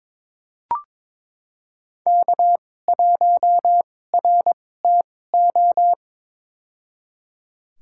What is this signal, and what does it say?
Morse code, heard as a keyed tone, about 22 wpm, spaced as for 16 wpm, 700 Hz. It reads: K1RTO
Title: USA Call Signs - 5 Characters - 2x Character Spacing - 22wpm